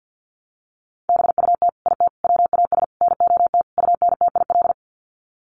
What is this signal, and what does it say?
Morse code, heard as a keyed tone, about 50 wpm, 700 Hz. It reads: BUT IT WAS NOT UNTIL
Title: Sets of 5 Words Encore2 50wpm